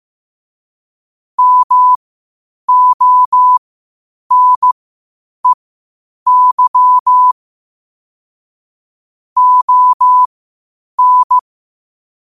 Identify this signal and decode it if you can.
Morse code, heard as a keyed tone, about 15 wpm, spaced as for 8 wpm, 1000 Hz. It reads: MONEY ON
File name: qotdfiles / 8wpm